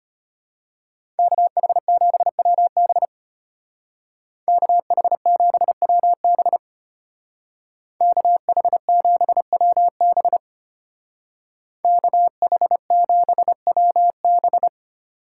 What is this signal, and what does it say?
Morse code, heard as a keyed tone, about 34 wpm, 700 Hz. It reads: KH7WB KH7WB KH7WB KH7WB